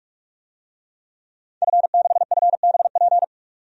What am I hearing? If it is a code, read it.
Morse code, heard as a keyed tone, about 46 wpm, 700 Hz. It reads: F6FBP
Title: DXCC-100 Call Signs - 5 Characters - 45wpm